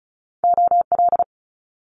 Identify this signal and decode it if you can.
Morse code, heard as a keyed tone, about 36 wpm, 700 Hz. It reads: OL
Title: Top Two Letter Combinations in Words 35wpm